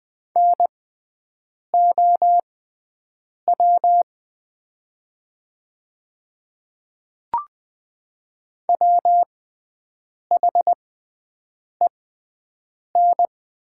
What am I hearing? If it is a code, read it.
Morse code, heard as a keyed tone, about 20 wpm, spaced as for 7 wpm, 700 Hz. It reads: NOW WHEN
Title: 6x Character Spacing - Top 100 Words - 20wpm